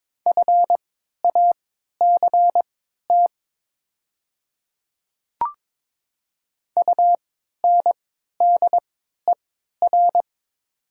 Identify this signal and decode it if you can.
Morse code, heard as a keyed tone, about 22 wpm, spaced as for 12 wpm, 700 Hz. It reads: FACT UNDER
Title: 3x Character Spacing - Top 500 Words - 22wpm